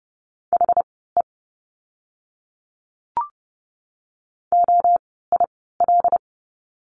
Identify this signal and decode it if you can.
Morse code, heard as a keyed tone, about 30 wpm, spaced as for 17 wpm, 700 Hz. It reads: HE OIL
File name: Lesson 16f - Letter c - Words 30wpm